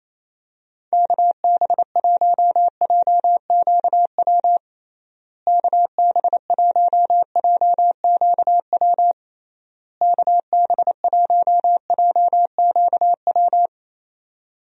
Morse code, heard as a keyed tone, about 27 wpm, 700 Hz. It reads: KB1JQW KB1JQW KB1JQW
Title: USA Call Signs - 6 Characters - Repeated 3x - 28wpm